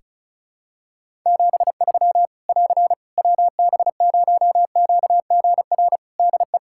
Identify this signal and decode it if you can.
Morse code, heard as a keyed tone, about 36 wpm, 700 Hz. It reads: 73 <AR> WB0QGR DE
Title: Practice QSO - Exchange 4 - Ending - 35wpm